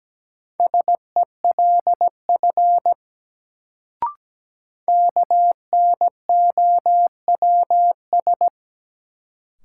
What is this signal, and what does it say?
Morse code, heard as a keyed tone, about 17 wpm, 700 Hz. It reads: SELF KNOWS
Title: Sentences from Top 1000 Words - Review Part 1 - New Word Drill - 17wpm